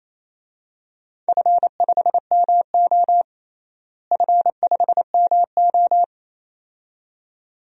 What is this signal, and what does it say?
Morse code, heard as a keyed tone, about 28 wpm, 700 Hz. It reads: F5MO F5MO
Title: DXCC-100 Call Signs - 4 Characters - Repeated 2x - 28wpm